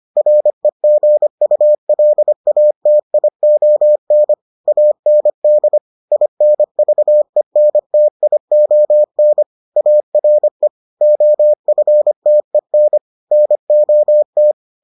Morse code, heard as a keyed tone, about 25 wpm, 600 Hz. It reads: REGULATION AND INVENTION ARE OFTEN NOT